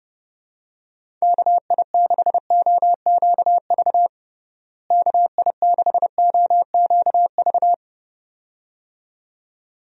Morse code, heard as a keyed tone, about 30 wpm, 700 Hz. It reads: KI6OQV KI6OQV
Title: USA Call Signs - 6 Characters - Repeated 2x - 30wpm